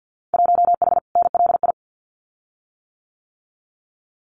Morse code, heard as a keyed tone, about 50 wpm, 700 Hz. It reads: 1H NLI
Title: Winter Field Day Exchanges 50wpm